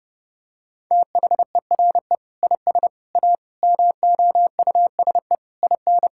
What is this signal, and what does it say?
Morse code, heard as a keyed tone, about 30 wpm, 700 Hz. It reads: THERE IS A MOUSE IN
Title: Sentences from Top 1000 Words - Review Part 5 - 30wpm